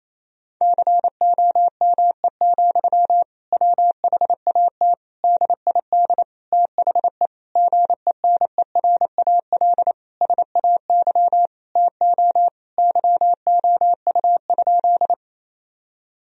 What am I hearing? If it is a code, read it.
Morse code, heard as a keyed tone, about 27 wpm, 700 Hz. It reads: COME, WHAT DID THE GENERAL SAY TO YOU?
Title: Sentences from Top 1000 Words - Review Part 3 - 28wpm